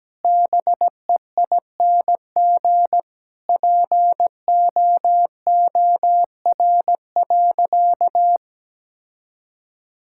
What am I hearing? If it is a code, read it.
Morse code, heard as a keyed tone, about 17 wpm, 700 Hz. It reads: BEING POOR.
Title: Sentences from Top 1000 Words - Review Part 6 - 17wpm